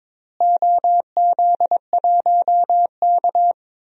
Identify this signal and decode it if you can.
Morse code, heard as a keyed tone, about 22 wpm, 700 Hz. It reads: OZ1K
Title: DXCC-100 Call Signs - 4 Characters - Repeated 3x - 22wpm